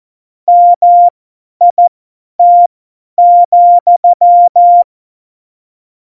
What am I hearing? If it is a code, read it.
Morse code, heard as a keyed tone, about 14 wpm, spaced as for 10 wpm, 700 Hz. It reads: MIT,